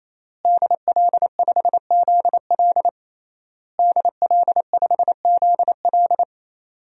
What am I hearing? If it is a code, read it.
Morse code, heard as a keyed tone, about 27 wpm, 700 Hz. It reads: DL5ZL DL5ZL
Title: DXCC-100 Call Signs - 5 Characters - Repeated 3x - 28wpm